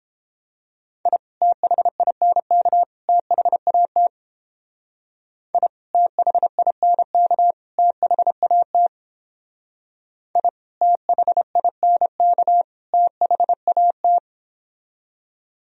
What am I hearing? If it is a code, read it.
Morse code, heard as a keyed tone, about 29 wpm, 700 Hz. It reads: I THINK THAT I THINK THAT I THINK THAT